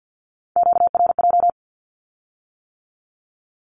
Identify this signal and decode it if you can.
Morse code, heard as a keyed tone, about 50 wpm, 700 Hz. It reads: QRJ
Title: Q-Codes with Spoken Meaning - 50wpm